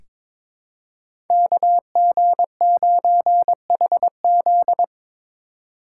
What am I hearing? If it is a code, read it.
Morse code, heard as a keyed tone, about 22 wpm, 700 Hz. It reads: KG9HZ
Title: USA Call Signs - 2x Character Spacing - 22wpm